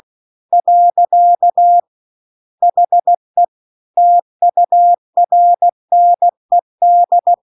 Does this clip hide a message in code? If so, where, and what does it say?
Yes: Morse code from 0.5 to 7.3 s, heard as a keyed tone, about 16 wpm, 700 Hz. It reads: . HE TURNED